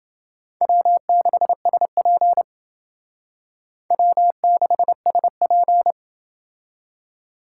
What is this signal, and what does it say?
Morse code, heard as a keyed tone, about 28 wpm, 700 Hz. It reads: W6SP W6SP